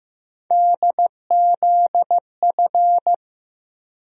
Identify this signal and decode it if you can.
Morse code, heard as a keyed tone, about 15 wpm, 700 Hz. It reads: DZF
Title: USA Call Sign Postfixes - 3 Letters 15wpm